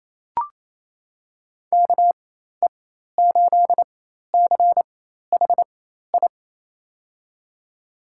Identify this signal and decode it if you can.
Morse code, heard as a keyed tone, about 28 wpm, spaced as for 13 wpm, 700 Hz. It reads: KE8CHI
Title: USA Call Signs - 6 Characters - 4x Character Spacing - 28wpm